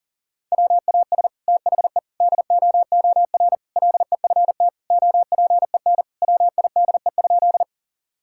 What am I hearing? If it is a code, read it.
Morse code, heard as a keyed tone, about 40 wpm, 700 Hz. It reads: WAS THE DOOR LEFT OPEN WIDE?
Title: Sentences from Top 1000 Words - Review Part 6 - 40wpm